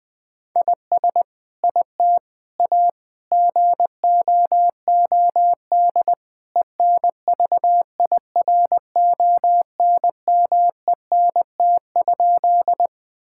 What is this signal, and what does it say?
Morse code, heard as a keyed tone, about 21 wpm, 700 Hz. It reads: IS IT A GOOD ENVIRONMENT?
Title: Sentences from Top 1000 Words - Review Part 5 - 20wpm